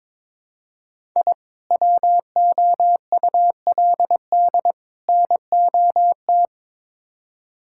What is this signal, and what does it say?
Morse code, heard as a keyed tone, about 22 wpm, 700 Hz. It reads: I WOULD NOT
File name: practice / 22wpm / Build A Phrase - 5-Word Phrases - 22wpm